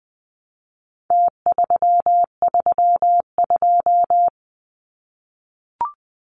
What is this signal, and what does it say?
Morse code, heard as a keyed tone, about 20 wpm, 700 Hz. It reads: T332
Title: Numbers - Contest Style - 4-Digit 20wpm